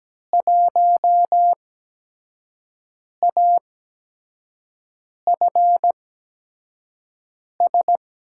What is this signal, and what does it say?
Morse code, heard as a keyed tone, about 17 wpm, 700 Hz. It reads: 1 A F S
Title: USA Call Signs - 5 Characters - 8x Character Spacing - 17wpm